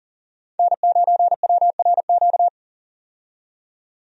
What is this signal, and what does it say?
Morse code, heard as a keyed tone, about 40 wpm, 700 Hz. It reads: N9WRQ
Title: USA Call Signs - 5 Characters - 40wpm